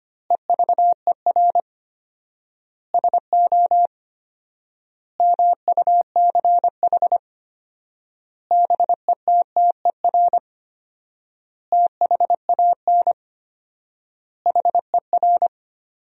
Morse code, heard as a keyed tone, about 25 wpm, 700 Hz. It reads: EVER SO MUCH BETTER THAN HER
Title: 4x Word-Spacing - Sentences from Top 400 Words - 25wpm